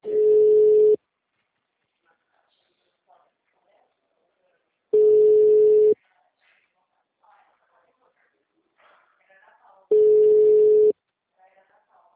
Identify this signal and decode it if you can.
Ringback tone (Europe, 425 Hz), with speech.